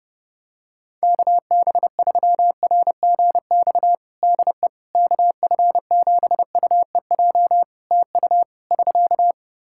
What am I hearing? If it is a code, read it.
Morse code, heard as a keyed tone, about 30 wpm, 700 Hz. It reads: KB3RGX DE KF7UEJ TU <SK>
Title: Practice QSO - Exchange 4 - Ending - 30wpm